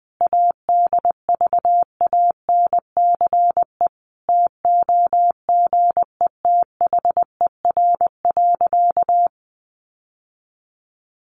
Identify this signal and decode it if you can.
Morse code, heard as a keyed tone, about 20 wpm, 700 Hz. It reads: ADVANCE TOGETHER.
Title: Sentences from Top 1400 Words 20wpm